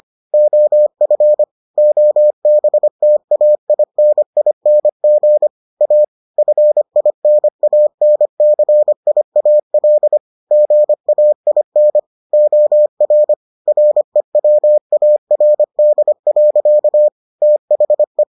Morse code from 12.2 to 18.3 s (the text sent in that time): OR REWARD. THE